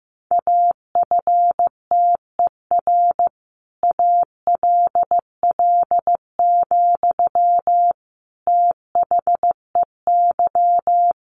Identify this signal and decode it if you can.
Morse code, heard as a keyed tone, about 15 wpm, 700 Hz. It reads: AFTER ALL, THEY
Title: Sentences from Top 1100 Words 15wpm